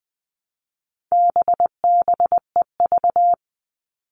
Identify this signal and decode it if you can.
Morse code, heard as a keyed tone, about 20 wpm, 700 Hz. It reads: BBEV
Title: All Cut Numbers - 4-Digit 20wpm